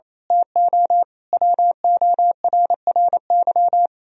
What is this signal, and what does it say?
Morse code, heard as a keyed tone, about 31 wpm, 700 Hz. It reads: TO WORRY
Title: Speed-Racing - Sets of 5 Words 28wpm